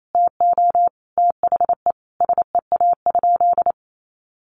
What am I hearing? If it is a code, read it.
Morse code, heard as a keyed tone, about 28 wpm, 700 Hz. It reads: TO THE SEA?
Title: Sentences from Top 300 Words 28wpm